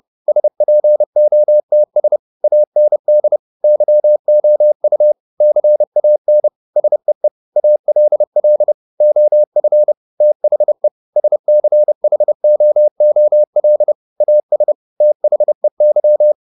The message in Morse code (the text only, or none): SPOTS AND YOU CAN SEE ALL OF THE SCHOOL AS THEY